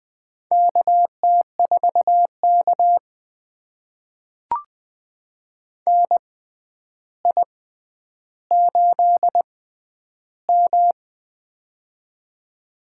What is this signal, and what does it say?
Morse code, heard as a keyed tone, about 20 wpm, spaced as for 10 wpm, 700 Hz. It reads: KT4K NI8M
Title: USA Call Signs - 4 Characters - 6x Character Spacing - 20wpm